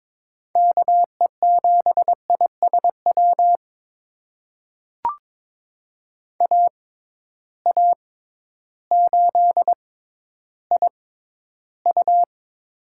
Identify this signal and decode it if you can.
Morse code, heard as a keyed tone, about 22 wpm, spaced as for 12 wpm, 700 Hz. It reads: KE7ISW AA8IU
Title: USA Call Signs - 6x Character Spacing - 22wpm